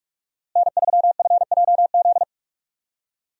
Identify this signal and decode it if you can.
Morse code, heard as a keyed tone, about 46 wpm, 700 Hz. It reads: N3FJZ
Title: USA Call Signs - 5 Characters - Repeated 2x - 45wpm